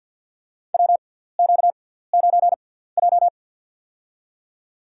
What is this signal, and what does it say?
Morse code, heard as a keyed tone, about 50 wpm, spaced as for 17 wpm, 700 Hz. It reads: WQ9J
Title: USA Call Signs - 6x Character Spacing - 50wpm